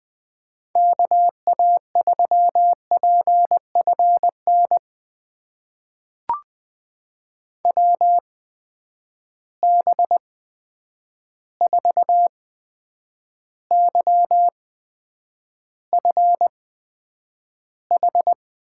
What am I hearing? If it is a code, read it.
Morse code, heard as a keyed tone, about 20 wpm, spaced as for 9 wpm, 700 Hz. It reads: KA3PFN WB4YFH
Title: USA Call Signs - 6 Characters - 8x Character Spacing - 20wpm